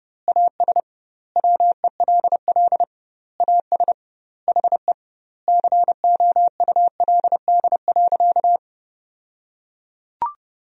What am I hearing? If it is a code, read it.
Morse code, heard as a keyed tone, about 30 wpm, 700 Hz. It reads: AS WELL AS HE COULD.
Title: 2x Word-Spacing - Sentences from Top 500 Words - 30wpm